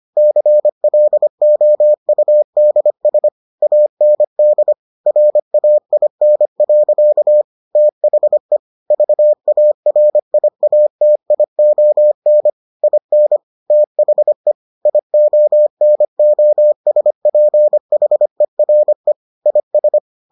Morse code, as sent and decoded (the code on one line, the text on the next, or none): -.-. .-.. --- ..- -.. ... / .- -. -.. / .-. .- .. -. .-.-.- / - .... / ...- .- .-. .. .- - .. --- -. / .. -. / - .... / .. --- -. --- ... .--. .... .-. / .. ...
CLOUDS AND RAIN. THE VARIATION IN THE IONOSPHERE IS